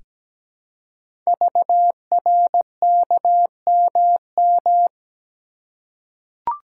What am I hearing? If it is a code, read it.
Morse code, heard as a keyed tone, about 17 wpm, 700 Hz. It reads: VRKMM
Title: Code Groups - 5 Letters - 17wpm